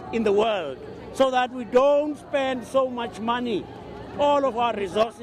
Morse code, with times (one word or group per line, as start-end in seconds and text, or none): none